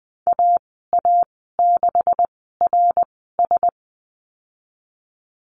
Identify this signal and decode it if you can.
Morse code, heard as a keyed tone, about 20 wpm, spaced as for 15 wpm, 700 Hz. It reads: AA6RS
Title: 167 Lesson 27g - Number 7 - Call Signs 20wpm